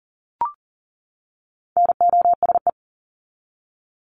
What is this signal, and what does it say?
Morse code, heard as a keyed tone, about 40 wpm, 700 Hz. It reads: NOSE
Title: Top 1501-1750 Words 40wpm